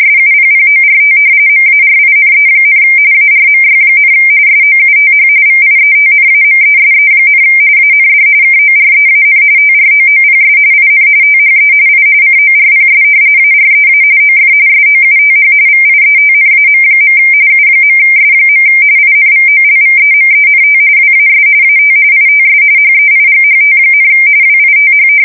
RTTY, about 45.45 baud. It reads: SUPPORT MARINE, ARMY, SOUTH VIETNAMESE, AND KOREAN TROOPS IN THE I CORPS AREA OF SOUTH VIETNAM. U.S. INVOLVEMENT WAS AT ITS HEIGHT, WITH
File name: fsk170 / battleshipnj